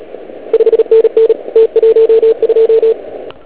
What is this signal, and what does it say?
Morse code, heard as a keyed tone, about 33 wpm, 450 Hz. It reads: 5NN T12